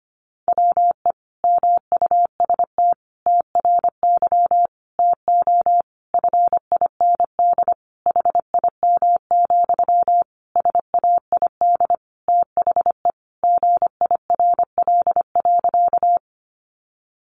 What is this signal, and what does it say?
Morse code, heard as a keyed tone, about 25 wpm, 700 Hz. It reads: WE MUST TRY TO FIND HIM, SAID THE GIRL.
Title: Sentences from Top 400 Words 25wpm